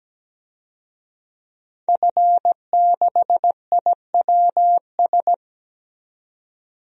Morse code, heard as a keyed tone, about 17 wpm, 700 Hz. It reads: F6IWS
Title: DXCC-100 Call Signs - 5 Characters - 17wpm